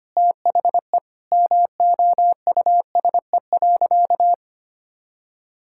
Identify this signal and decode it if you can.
Morse code, heard as a keyed tone, about 25 wpm, 700 Hz. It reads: THE MOUSE.
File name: practice / 25wpm / All-Age Appropriate Jokes - 25wpm